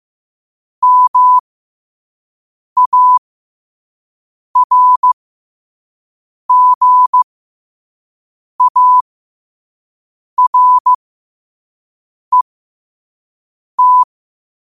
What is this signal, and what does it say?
Morse code, heard as a keyed tone, about 15 wpm, spaced as for 5 wpm, 1000 Hz. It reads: MARGARET
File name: qotdfiles / 5wpm